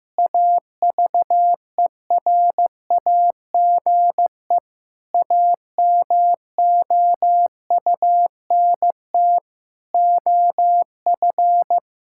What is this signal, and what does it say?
Morse code, heard as a keyed tone, about 15 wpm, 700 Hz. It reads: AVERAGE AMOUNT OF
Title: Sentences from Top 1000 Words - Review Part 6 - 15wpm